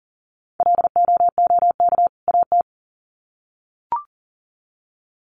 Morse code, heard as a keyed tone, about 40 wpm, 700 Hz. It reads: LOOK AT
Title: Sets of 2 Words 40wpm